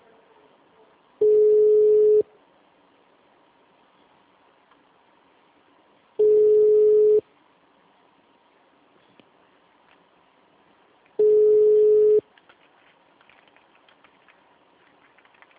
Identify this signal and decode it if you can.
Ringback tone (Europe, 425 Hz).
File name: out- -2011- - - .1499